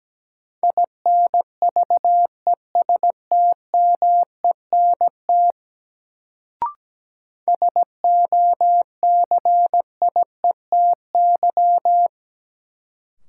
Morse code, heard as a keyed tone, about 17 wpm, 700 Hz. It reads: INVESTMENT SOCIETY
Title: Sentences from Top 700 Words - New Word Drill - 17wpm